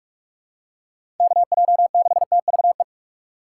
Morse code, heard as a keyed tone, about 47 wpm, 700 Hz. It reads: KJ6TVE
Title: USA Call Signs - 6 Characters - 3x Character Spacing - 45wpm